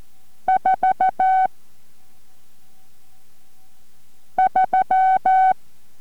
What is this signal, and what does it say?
Morse code, heard as a keyed tone, about 14 wpm, 750 Hz. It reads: 4 3